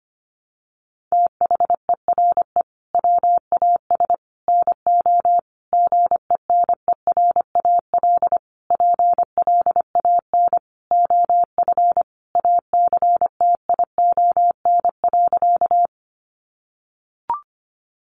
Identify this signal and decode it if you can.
Morse code, heard as a keyed tone, about 25 wpm, 700 Hz. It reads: THERE WAS NO GENERAL PLAN OF ACTION.
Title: Sentences from Top 700 Words 25wpm